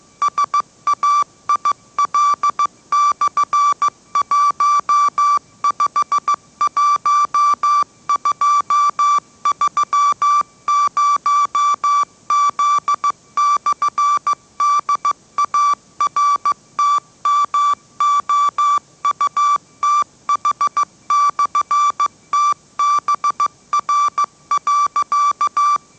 Morse code, heard as a keyed tone, about 15 wpm, 1160 Hz. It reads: SAIL/151230Z/DARTMOUTH/TBR.
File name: mrx / mrx065 16wpm